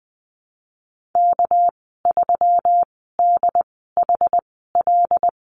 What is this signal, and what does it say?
Morse code, heard as a keyed tone, about 20 wpm, spaced as for 15 wpm, 700 Hz. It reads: K3DHL
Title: 207 Lesson 33g - Letter j - Call Signs 20wpm